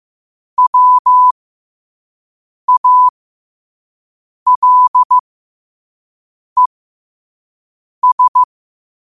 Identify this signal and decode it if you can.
Morse code, heard as a keyed tone, about 15 wpm, spaced as for 5 wpm, 1000 Hz. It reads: WALES
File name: 5wpm